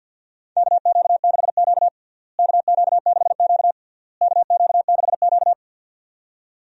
Morse code, heard as a keyed tone, about 51 wpm, 700 Hz. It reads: KQ6Q KQ6Q KQ6Q
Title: USA Call Signs - 4 Characters - Repeated 3x - 50wpm